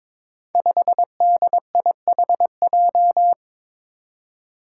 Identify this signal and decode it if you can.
Morse code, heard as a keyed tone, about 22 wpm, 700 Hz. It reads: 5DIHJ
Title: Five Challenging Letter-Number Combinations 22wpm